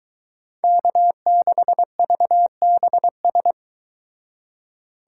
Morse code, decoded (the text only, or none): K6VBS